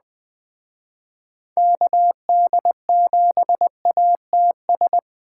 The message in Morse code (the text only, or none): KD7ATS